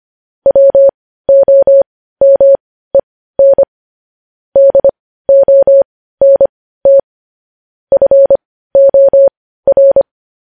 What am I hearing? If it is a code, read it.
Morse code, heard as a keyed tone, about 25 wpm, spaced as for 15 wpm, 550 Hz. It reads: WOMEN DONT FOR
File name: podcast / News headlines-1x-25x15